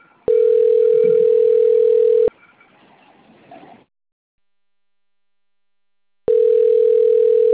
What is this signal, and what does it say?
Ringback tone (North America), 440 + 480 Hz.